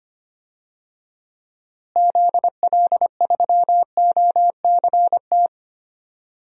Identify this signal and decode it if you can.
Morse code, heard as a keyed tone, about 25 wpm, 700 Hz. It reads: ZL3OCT
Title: DXCC-100 Call Signs - 6 Characters - Repeated 3x - 25wpm